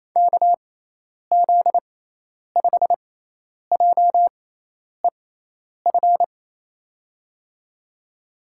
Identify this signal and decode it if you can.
Morse code, heard as a keyed tone, about 28 wpm, spaced as for 10 wpm, 700 Hz. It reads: KZ5JEF
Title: USA Call Signs - 6x Character Spacing - 28wpm